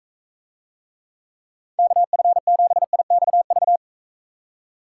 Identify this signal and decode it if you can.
Morse code, heard as a keyed tone, about 45 wpm, 700 Hz. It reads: KF7IXV